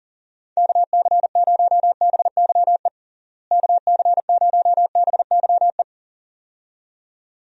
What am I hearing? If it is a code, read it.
Morse code, heard as a keyed tone, about 37 wpm, 700 Hz. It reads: KC0BYE KC0BYE
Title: USA Call Signs - 6 Characters - Repeated 2x - 40wpm